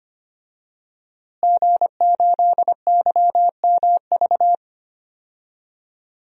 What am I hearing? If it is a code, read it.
Morse code, heard as a keyed tone, about 25 wpm, 700 Hz. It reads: G8YMV
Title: DXCC-100 Call Signs - 5 Characters - Repeated 2x - 25wpm